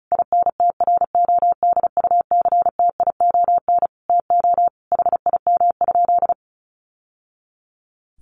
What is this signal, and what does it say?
Morse code, heard as a keyed tone, about 34 wpm, 700 Hz. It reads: INTRODUCTION TO HIM?
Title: Sentences from Top 1000 Words 35wpm